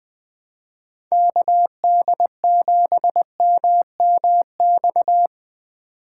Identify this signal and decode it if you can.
Morse code, heard as a keyed tone, about 20 wpm, 700 Hz. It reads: KD7MMX
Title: USA Call Signs - 2x Character Spacing - 20wpm